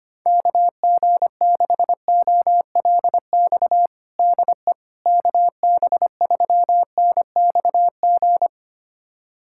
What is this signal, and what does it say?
Morse code, heard as a keyed tone, about 25 wpm, 700 Hz. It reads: KG6OLX DE KB3NXG
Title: USA Callsign DE Callsign - 6 Characters - 25wpm